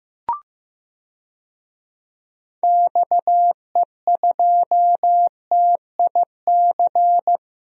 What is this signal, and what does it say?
Morse code, heard as a keyed tone, about 15 wpm, 700 Hz. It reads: XE2TIC
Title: DXCC-100 Call Signs - 6 Characters - 15wpm